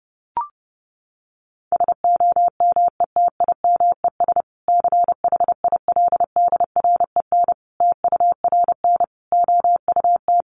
Morse code, heard as a keyed tone, about 30 wpm, 700 Hz. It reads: SOMETIMES CHILDREN TURN OUT
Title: Sentences from Top 600 Words 30wpm